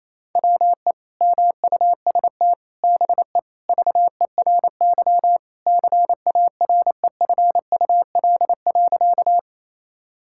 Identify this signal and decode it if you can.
Morse code, heard as a keyed tone, about 28 wpm, 700 Hz. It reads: WE MUST BE VERY CAREFUL.
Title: Sentences from Top 1600 Words 28wpm